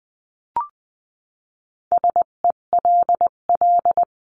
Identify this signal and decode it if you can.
Morse code, heard as a keyed tone, about 20 wpm, 700 Hz. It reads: SELL
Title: 258 Lesson 42c - Words 16-20wpm